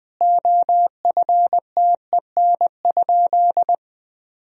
Morse code, heard as a keyed tone, about 20 wpm, 700 Hz. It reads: OFTEN?
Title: Sentences from Top 1000 Words - Review Part 4 - 20wpm